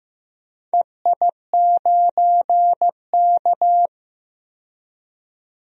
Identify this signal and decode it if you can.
Morse code, heard as a keyed tone, about 15 wpm, 700 Hz. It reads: EI9K
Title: DXCC-100 Call Signs - 4 Characters - 15wpm